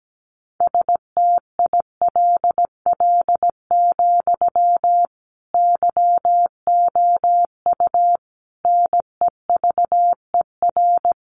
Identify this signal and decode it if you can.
Morse code, heard as a keyed tone, about 17 wpm, 700 Hz. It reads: STILL, YOU NEVER
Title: Sentences from Top 300 Words 17wpm